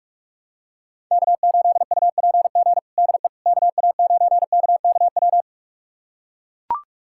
Morse code, heard as a keyed tone, about 44 wpm, 700 Hz. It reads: K8UPG DE KA9KKW
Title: USA Callsign DE Callsign 45wpm